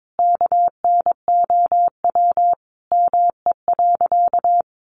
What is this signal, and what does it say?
Morse code, heard as a keyed tone, about 22 wpm, 700 Hz. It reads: KNOW ME.
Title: Sentences from Top 600 Words 22wpm